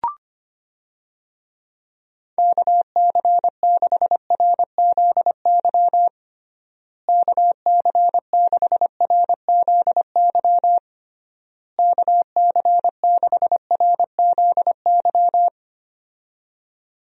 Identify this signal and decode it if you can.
Morse code, heard as a keyed tone, about 25 wpm, 700 Hz. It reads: KC6RZY KC6RZY KC6RZY
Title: USA Call Signs - 6 Characters - Repeated 3x - 25wpm